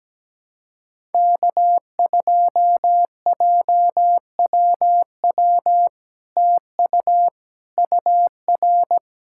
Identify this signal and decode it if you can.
Morse code, heard as a keyed tone, about 17 wpm, 700 Hz. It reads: K2JWW TU UR
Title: SOTA QSO Practice 17wpm